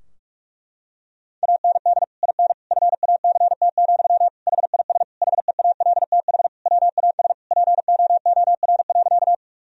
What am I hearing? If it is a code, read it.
Morse code, heard as a keyed tone, about 41 wpm, 700 Hz. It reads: AND IN FACT, HIS HEALTH WAS POOR.
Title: Sentences from Top 1000 Words - Review Part 3 - 45wpm